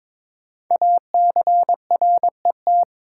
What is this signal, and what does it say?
Morse code, heard as a keyed tone, about 23 wpm, 700 Hz. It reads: ACRET
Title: Code Groups - 5 Letters - 22wpm